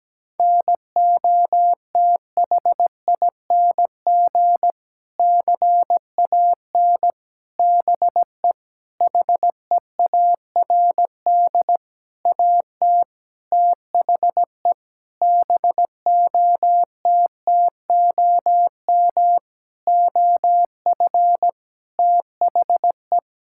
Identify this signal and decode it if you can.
Morse code, heard as a keyed tone, about 17 wpm, 700 Hz. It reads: NOTHING CAN BE HEARD AT THE BOTTOM OF THE
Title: Sentences from Top 1000 Words - Review Part 6 - 17wpm